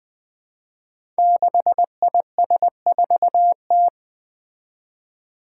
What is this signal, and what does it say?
Morse code, heard as a keyed tone, about 20 wpm, 700 Hz. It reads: 6IS4T
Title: Five Challenging Letter-Number Combinations 20wpm